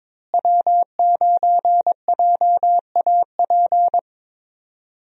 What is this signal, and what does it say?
Morse code, heard as a keyed tone, about 22 wpm, 700 Hz. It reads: W9JAP
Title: USA Call Signs - 5 Characters - Repeated 3x - 22wpm